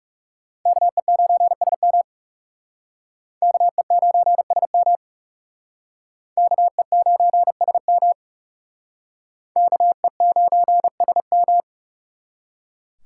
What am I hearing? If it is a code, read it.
Morse code, heard as a keyed tone, about 39 wpm, 700 Hz. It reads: KE9SM KE9SM KE9SM KE9SM